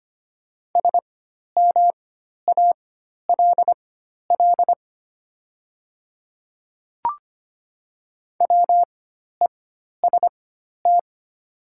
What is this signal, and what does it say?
Morse code, heard as a keyed tone, about 25 wpm, spaced as for 12 wpm, 690 Hz. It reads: SMALL WEST
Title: 4x Character Spacing - Top 500 Words - 25wpm